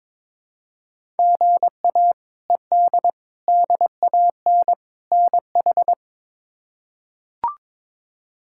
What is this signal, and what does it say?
Morse code, heard as a keyed tone, about 22 wpm, 700 Hz. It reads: GA ED DAN NH
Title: K1USN SST QSO Practice - 22wpm